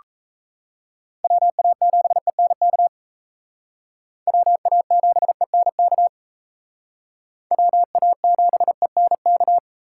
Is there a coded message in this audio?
Yes: Morse code from 1.2 to 9.6 s, heard as a keyed tone, about 35 wpm, 700 Hz. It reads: WA7ENK WA7ENK WA7ENK